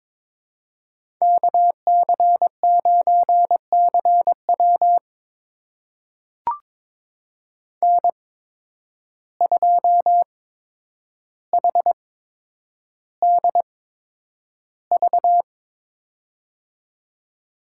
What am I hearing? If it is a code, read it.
Morse code, heard as a keyed tone, about 22 wpm, spaced as for 9 wpm, 700 Hz. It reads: KC9CW N2HDV
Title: USA Call Signs - 5 Characters - 8x Character Spacing - 22wpm